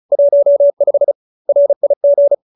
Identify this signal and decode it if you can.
Morse code, heard as a keyed tone, about 34 wpm, 570 Hz. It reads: 15 RIG